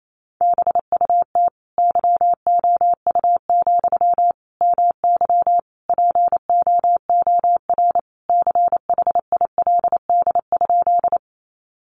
Morse code, heard as a keyed tone, about 28 wpm, 700 Hz. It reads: BUT YOU, MY POOR CHILD?
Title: Sentences from Top 1000 Words 28wpm